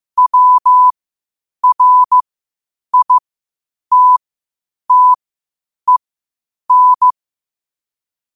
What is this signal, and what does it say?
Morse code, heard as a keyed tone, about 15 wpm, spaced as for 8 wpm, 1000 Hz. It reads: WRITTEN
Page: New quotes every day in morse code at 8 Words per minute.